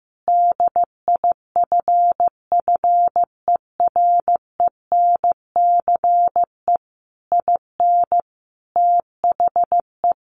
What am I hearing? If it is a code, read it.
Morse code, heard as a keyed tone, about 14 wpm, 700 Hz. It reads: DIFFERENCE IN THE
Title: Sentences from Top 800 Words 15wpm